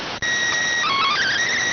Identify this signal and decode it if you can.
SSTV (slow-scan television), PD 290.